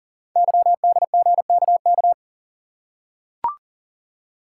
Morse code, heard as a keyed tone, about 43 wpm, 700 Hz. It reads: YDGKK
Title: Code Groups - 5 Letters - 40wpm